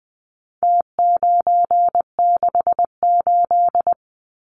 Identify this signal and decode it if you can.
Morse code, heard as a keyed tone, about 20 wpm, 700 Hz. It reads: T968
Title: Numbers - Contest Style - 4-Digit 20wpm